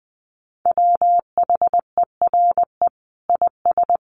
Morse code, heard as a keyed tone, about 19 wpm, 700 Hz. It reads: WHERE IS
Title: Sentences from Top 1500 Words 20wpm